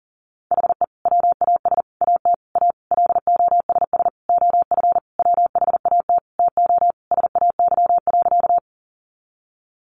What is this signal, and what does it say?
Morse code, heard as a keyed tone, about 40 wpm, 700 Hz. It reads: HE WAS AT A LOSS OF WHAT TO SAY.
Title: Sentences from Top 1400 Words 40wpm